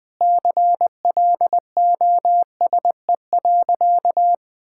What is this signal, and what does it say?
Morse code, heard as a keyed tone, about 19 wpm, 700 Hz. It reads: CLOSE.
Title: Sentences with 5-Word Phrases - Spoken Beginning - 20wpm